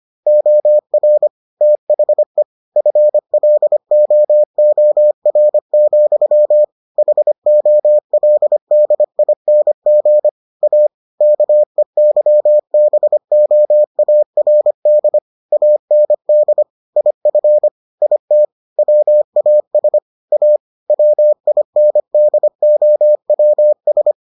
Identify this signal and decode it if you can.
Morse code, heard as a keyed tone, about 25 wpm, 600 Hz. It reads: OR THE FLOOR, HOLDING A KEYBOARD AND IF IT WAS A WINDOWS